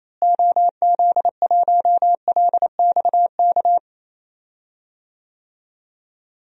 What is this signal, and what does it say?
Morse code, heard as a keyed tone, about 27 wpm, 700 Hz. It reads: OZ1LXK
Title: DXCC-100 Call Signs - 6 Characters - Repeated 2x - 28wpm